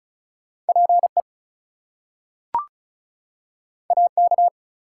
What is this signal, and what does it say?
Morse code, heard as a keyed tone, about 36 wpm, 700 Hz. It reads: PE AK
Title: US States and Canadian Provinces 35wpm